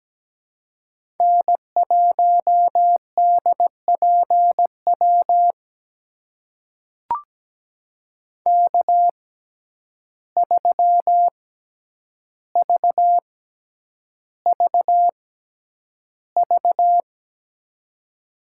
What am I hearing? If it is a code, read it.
Morse code, heard as a keyed tone, about 17 wpm, spaced as for 9 wpm, 700 Hz. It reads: N1DPW K3VVV
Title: USA Call Signs - 5 Characters - 6x Character Spacing - 17wpm